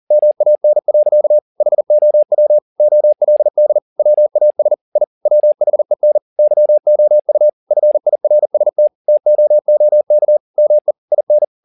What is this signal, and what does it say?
Morse code, heard as a keyed tone, about 40 wpm, 600 Hz. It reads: MAN. HOW OLD WAS I WHEN YOU FIRST TOOK ME IN